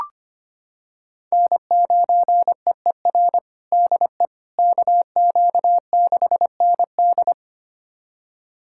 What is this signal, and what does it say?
Morse code, heard as a keyed tone, about 25 wpm, 700 Hz. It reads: N9EER DE KQ6ND
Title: USA Callsign DE Callsign - 5 Characters - 25wpm